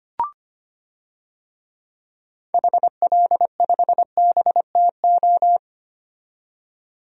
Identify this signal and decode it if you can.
Morse code, heard as a keyed tone, about 25 wpm, 700 Hz. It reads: HL5BTO
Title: DXCC-100 Call Signs - 6 Characters - 25wpm